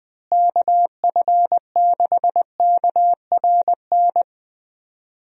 Morse code, heard as a keyed tone, about 20 wpm, 700 Hz. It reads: KF6KRN
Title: USA Call Signs - 6 Characters - Repeated 2x - 20wpm